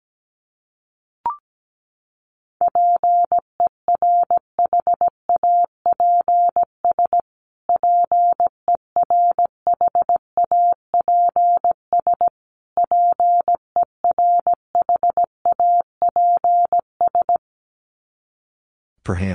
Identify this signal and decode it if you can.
Morse code, heard as a keyed tone, about 17 wpm, 700 Hz. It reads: PERHAPS PERHAPS PERHAPS
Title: Top 500 Words Repeated 17wpm